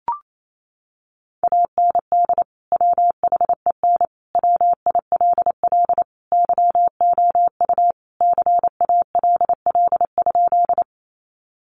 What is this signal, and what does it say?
Morse code, heard as a keyed tone, about 28 wpm, 700 Hz. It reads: AND WHEN WILL YOU CALL?
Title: Sentences from Top 100 Words 28wpm